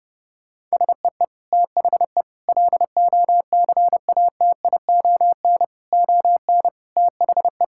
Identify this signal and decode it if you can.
Morse code, heard as a keyed tone, about 32 wpm, 700 Hz. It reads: SEE THE LOCATION ON THE
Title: Sentences from Top 1000 Words - Review Part 6 - 30wpm